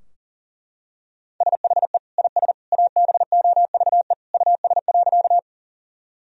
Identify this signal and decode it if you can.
Morse code, heard as a keyed tone, about 40 wpm, 700 Hz. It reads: SHE IS ABOVE US.
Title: Sentences from Top 1000 Words - Review Part 2 - 40wpm